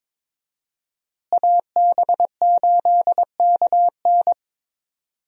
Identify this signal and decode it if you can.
Morse code, heard as a keyed tone, about 22 wpm, 700 Hz. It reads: AB8KN
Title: USA Call Signs - 5 Characters - Repeated 2x - 22wpm